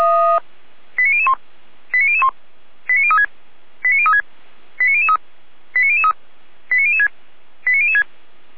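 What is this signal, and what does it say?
Five-tone selective call (ZVEI), heard as quick runs of five tones, tones of about 70 ms. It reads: 89001 89001 89036 89036 89003 89003 89006 89006
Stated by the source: ZVEI five-tone call received by SDR, from the feuerwehr-orlamuende.org fire brigade audio server.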